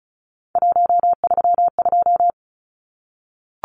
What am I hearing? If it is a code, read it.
Morse code, heard as a keyed tone, about 34 wpm, 700 Hz. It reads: 132
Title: Numbers - 3-Digit 35wpm